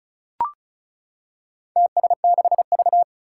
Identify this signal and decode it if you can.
Morse code, heard as a keyed tone, about 36 wpm, 700 Hz. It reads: TS6V
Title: Four Challenging Letter-Number Combinations 35wpm